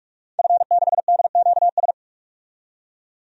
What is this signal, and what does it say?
Morse code, heard as a keyed tone, about 42 wpm, 700 Hz. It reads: F6DQS
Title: DXCC-100 Call Signs - 5 Characters - 45wpm